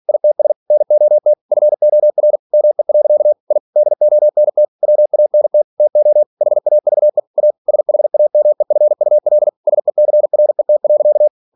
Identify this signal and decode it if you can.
Morse code, heard as a keyed tone, about 46 wpm, 600 Hz. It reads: ITS NOT FOR ME. I DONT WANT TO HAVE A SHAMEFUL SECRET.
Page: Never 47wpm